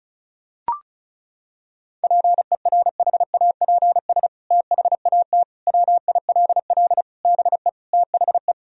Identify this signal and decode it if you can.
Morse code, heard as a keyed tone, about 36 wpm, 700 Hz. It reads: PERHAPS THAT WILL BE THE
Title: Sentences from Top 1000 Words - Review Part 2 - 35wpm